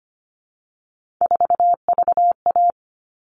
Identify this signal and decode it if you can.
Morse code, heard as a keyed tone, about 25 wpm, 700 Hz. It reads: 4VA
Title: All Cut Numbers - 3-Digit 25wpm